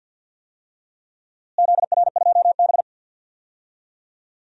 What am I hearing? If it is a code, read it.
Morse code, heard as a keyed tone, about 52 wpm, 700 Hz. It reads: ZR2B
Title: DXCC-100 Call Signs - 4 Characters - Repeated 3x - 50wpm